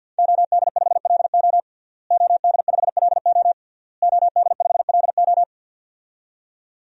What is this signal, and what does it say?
Morse code, heard as a keyed tone, about 52 wpm, 700 Hz. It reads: OD5LO OD5LO OD5LO
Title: DXCC-100 Call Signs - 5 Characters - Repeated 3x - 50wpm